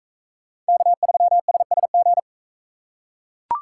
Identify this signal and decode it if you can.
Morse code, heard as a keyed tone, about 40 wpm, 700 Hz. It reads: K3SSG